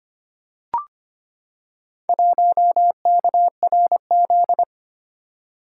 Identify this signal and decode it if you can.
Morse code, heard as a keyed tone, about 25 wpm, 700 Hz. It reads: 1KRZ
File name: practice / 25wpm / Four Letter-Number Combinations - 25wpm